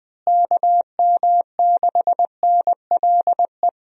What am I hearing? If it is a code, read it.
Morse code, heard as a keyed tone, about 20 wpm, 700 Hz. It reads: KM6NLE
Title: USA Call Signs - 6 Characters - Repeated 3x - 20wpm